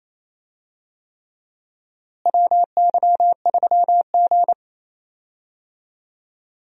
Morse code, heard as a keyed tone, about 28 wpm, 700 Hz. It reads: WY3G
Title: USA Call Signs - 4 Characters - Repeated 3x - 28wpm